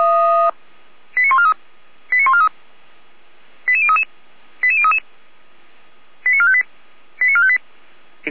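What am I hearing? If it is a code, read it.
Five-tone selective call (ZVEI), heard as quick runs of five tones, tones of about 70 ms. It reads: 89143 89143 88033 88033 89458 89458
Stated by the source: ZVEI five-tone selective call received by SDR for the Feuerwehr Orlamünde fire service.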